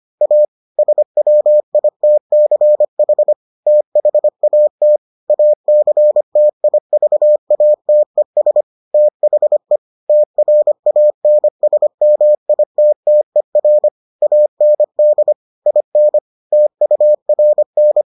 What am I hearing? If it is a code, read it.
Morse code, heard as a keyed tone, about 25 wpm, 600 Hz. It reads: A SWITCH THAT ACTIVATES THE TRANSMITTER AND IN TURN